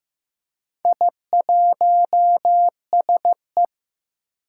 Morse code, heard as a keyed tone, about 15 wpm, 700 Hz. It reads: I1SE
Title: Four Challenging Letter-Number Combinations 15wpm